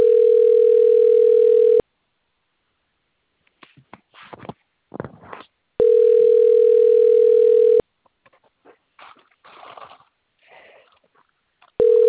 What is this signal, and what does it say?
Ringback tone (North America), 440 + 480 Hz.